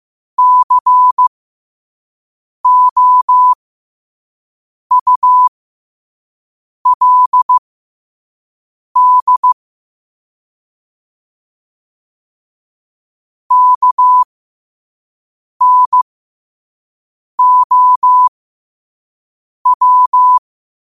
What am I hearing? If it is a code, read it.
Morse code, heard as a keyed tone, about 15 wpm, spaced as for 5 wpm, 1000 Hz. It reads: COULD KNOW